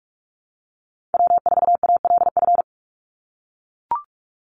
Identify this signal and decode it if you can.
Morse code, heard as a keyed tone, about 44 wpm, 700 Hz. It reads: W4ALF
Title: Call Signs - 45wpm